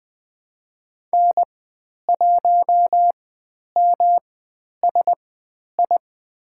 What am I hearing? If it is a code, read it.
Morse code, heard as a keyed tone, about 20 wpm, spaced as for 10 wpm, 700 Hz. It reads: N1MSI
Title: 109 Lesson 18g - Letter m - Call Signs 20wpm